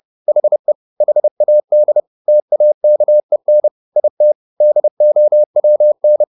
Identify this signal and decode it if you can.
Morse code, heard as a keyed tone, about 30 wpm, 600 Hz. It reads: HE HAD TAKEN IT DOWN
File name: the old man and the sea / the old man and the sea 1 30wpm